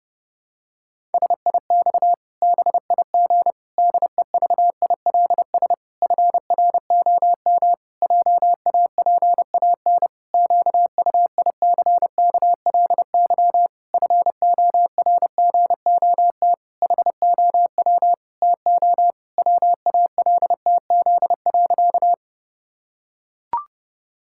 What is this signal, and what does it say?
Morse code, heard as a keyed tone, about 30 wpm, 700 Hz. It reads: SIX BIG DEVILS FROM JAPAN QUICKLY FORGOT HOW TO WALTZ.
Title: Pangram Challenge v2 - 30wpm